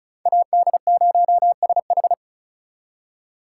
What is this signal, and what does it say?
Morse code, heard as a keyed tone, about 44 wpm, 700 Hz. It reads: AD0SH
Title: Speed Racing - USA Call Signs 35wpm